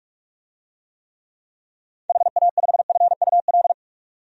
Morse code, heard as a keyed tone, about 46 wpm, 700 Hz. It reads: HA5FUL
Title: DXCC-100 Call Signs - 6 Characters - 45wpm